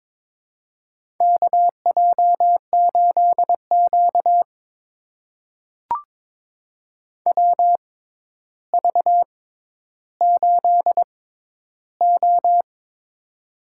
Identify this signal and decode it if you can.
Morse code, heard as a keyed tone, about 23 wpm, spaced as for 11 wpm, 700 Hz. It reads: KJ8Q WV8O
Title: USA Call Signs - 4 Characters - 6x Character Spacing - 22wpm